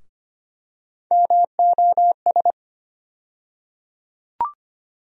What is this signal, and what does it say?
Morse code, heard as a keyed tone, about 25 wpm, 700 Hz. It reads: MOS
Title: USA Call Sign Postfixes - 3 Letters 25wpm